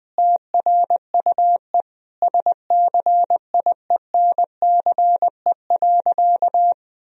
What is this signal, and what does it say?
Morse code, heard as a keyed tone, about 20 wpm, 700 Hz. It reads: TRUE SCIENCE.
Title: Sentences from Top 1000 Words - Review Part 4 - 20wpm